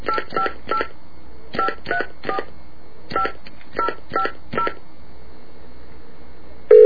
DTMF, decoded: 858 534 5050